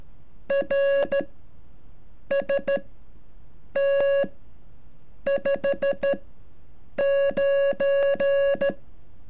Morse code, heard as a keyed tone, about 14 wpm, spaced as for 7 wpm, 560 Hz. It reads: RST 59
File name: curso / ejercicio 33